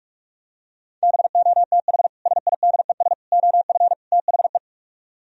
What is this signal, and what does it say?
Morse code, heard as a keyed tone, about 47 wpm, 700 Hz. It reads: BOTH SIDES OF THE